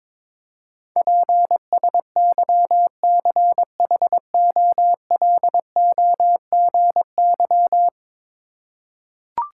Morse code, heard as a keyed tone, about 22 wpm, 700 Hz. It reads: PSYCHOLOGY
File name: practice / 22wpm / 10 Letter Words - First 3 Letters Given - 22wpm